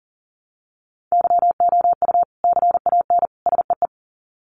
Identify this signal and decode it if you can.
Morse code, heard as a keyed tone, about 40 wpm, 700 Hz. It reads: YOU CAN SEE
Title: Sets of 3 Words 40wpm